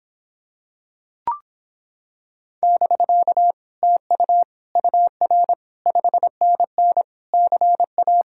Morse code, heard as a keyed tone, about 26 wpm, 700 Hz. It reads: <BK> TU UR 5NN CA